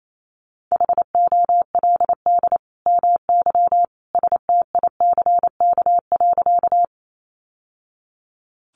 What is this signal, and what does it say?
Morse code, heard as a keyed tone, about 28 wpm, 700 Hz. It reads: HOLD MY STICK.
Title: Sentences from Top 1200 Words 28wpm